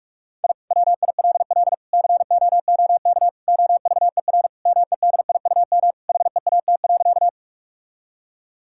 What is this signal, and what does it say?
Morse code, heard as a keyed tone, about 41 wpm, 700 Hz. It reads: I WILL COOK OVER MEDIUM HEAT.
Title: Sentences from Top 1000 Words - Review Part 6 - 45wpm